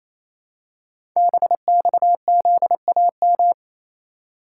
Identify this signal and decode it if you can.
Morse code, heard as a keyed tone, about 27 wpm, 700 Hz. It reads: BXZAM
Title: Code Groups - 5 Letters - 28wpm